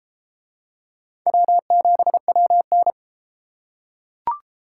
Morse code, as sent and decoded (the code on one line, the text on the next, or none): .-- --... .-- -.
W7WN